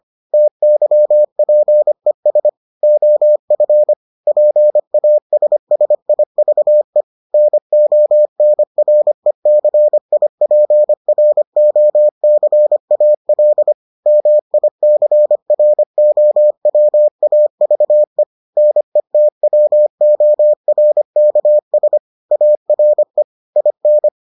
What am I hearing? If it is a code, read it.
Morse code, heard as a keyed tone, about 25 wpm, 600 Hz. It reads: TYPES OF PASSIVE NONRECIPROCAL MICROWAVE NETWORKS ARE IN